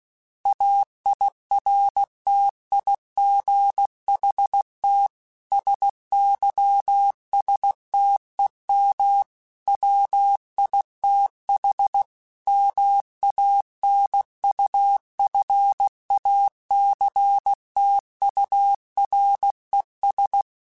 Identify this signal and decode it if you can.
Morse code, heard as a keyed tone, about 16 wpm, 800 Hz. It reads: AIRTIGHT SYSTEM WITH MANUFACTURES